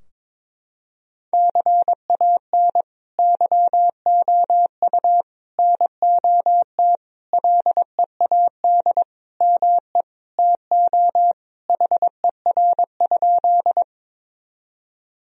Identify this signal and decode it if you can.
Morse code, heard as a keyed tone, about 22 wpm, 700 Hz. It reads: CAN YOU NOT LEAD ME TO HER?
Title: Sentences from Top 1000 Words - Review Part 2 - 22wpm